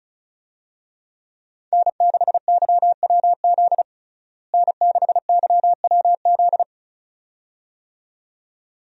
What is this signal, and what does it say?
Morse code, heard as a keyed tone, about 36 wpm, 700 Hz. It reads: N6YWZ N6YWZ
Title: USA Call Signs - 5 Characters - Repeated 2x - 35wpm